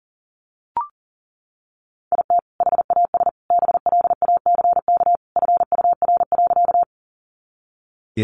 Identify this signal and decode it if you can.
Morse code, heard as a keyed tone, about 40 wpm, 700 Hz. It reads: IT HAS BLACK FUR.
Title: Easy Phrases 40wpm